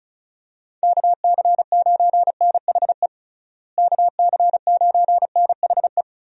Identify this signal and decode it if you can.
Morse code, heard as a keyed tone, about 36 wpm, 700 Hz. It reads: KC9NHE KC9NHE
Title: USA Call Signs - 6 Characters - Repeated 3x - 35wpm